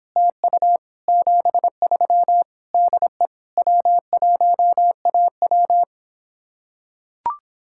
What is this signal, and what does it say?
Morse code, heard as a keyed tone, about 26 wpm, 700 Hz. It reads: TU 73 DE W1AW